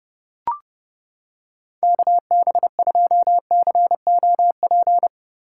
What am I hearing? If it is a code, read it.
Morse code, heard as a keyed tone, about 28 wpm, 700 Hz. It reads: KB2COP